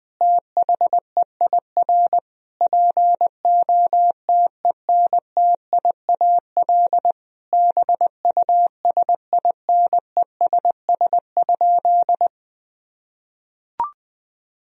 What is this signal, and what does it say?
Morse code, heard as a keyed tone, about 21 wpm, 700 Hz. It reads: THEIR POTENTIAL BUSINESS?
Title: Sentences from Top 1000 Words - Review Part 4 - 20wpm